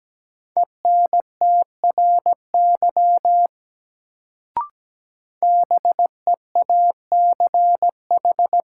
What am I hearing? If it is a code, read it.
Morse code, heard as a keyed tone, about 17 wpm, 700 Hz. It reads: ENTRY BEACH
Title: Sentences from Top 1500 Words - New Word Drill Alt - 17wpm